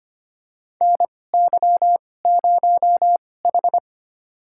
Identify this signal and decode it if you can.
Morse code, heard as a keyed tone, about 25 wpm, spaced as for 18 wpm, 700 Hz. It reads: NY0H
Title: USA Call Signs - 4 Characters - 2x Character Spacing - 25wpm